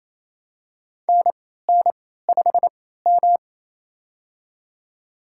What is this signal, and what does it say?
Morse code, heard as a keyed tone, about 27 wpm, spaced as for 16 wpm, 700 Hz. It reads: NN5M
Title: USA Call Signs - 4 Characters - 3x Character Spacing - 28wpm